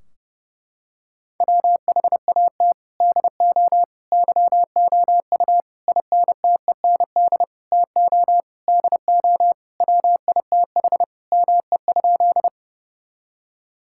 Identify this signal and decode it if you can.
Morse code, heard as a keyed tone, about 30 wpm, 700 Hz. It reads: WHAT DO YOU INTEND TO DO WITH ME?
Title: Sentences from Top 1600 Words 30wpm